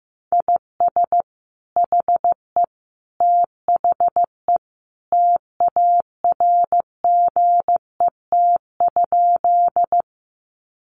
Morse code, heard as a keyed tone, about 15 wpm, 700 Hz. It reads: IS HE THE TARGET?
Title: Sentences from Top 1200 Words 15wpm